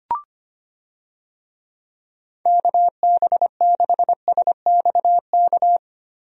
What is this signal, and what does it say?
Morse code, heard as a keyed tone, about 25 wpm, 700 Hz. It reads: KB6SXK
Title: USA Call Signs - 6 Characters - Repeated 2x - 25wpm